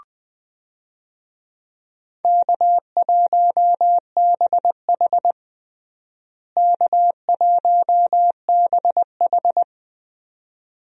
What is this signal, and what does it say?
Morse code, heard as a keyed tone, about 21 wpm, 700 Hz. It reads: K1BH K1BH
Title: USA Call Signs - 4 Characters - Repeated 2x - 20wpm